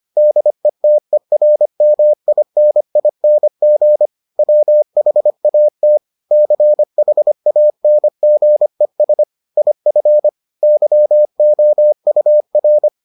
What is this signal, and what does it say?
Morse code, heard as a keyed tone, about 25 wpm, 600 Hz. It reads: DETERMINING WHAT CHANGES IF YOUR